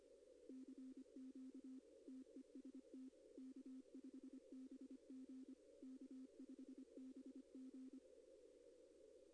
Morse code, heard as a keyed tone, about 25 wpm, 260 Hz. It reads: CQ TEST K5BG K5BG
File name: contests / arrl-160-k5tr-op-w5tn -2200